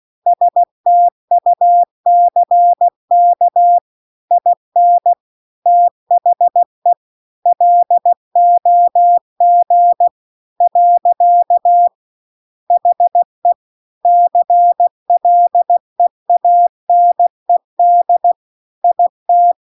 Morse code, heard as a keyed tone, about 16 wpm, 700 Hz. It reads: STUCK IN THE LOG . HE CLEANED IT
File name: CW-The-Big-Twoheaded-River-2-16WPM